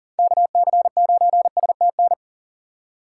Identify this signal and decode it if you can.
Morse code, heard as a keyed tone, about 37 wpm, 700 Hz. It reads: KC9STN
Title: USA Call Signs - 6x Character Spacing - 40wpm